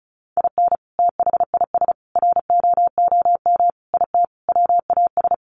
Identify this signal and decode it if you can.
Morse code, heard as a keyed tone, about 34 wpm, 700 Hz. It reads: IN THIS ROOM IT WAS
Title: Sentences from Top 700 Words 35wpm